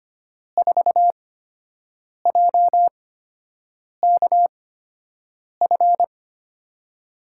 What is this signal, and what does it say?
Morse code, heard as a keyed tone, about 25 wpm, spaced as for 7 wpm, 700 Hz. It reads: 4JKF
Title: USA Call Signs - 6 Characters - 8x Character Spacing - 25wpm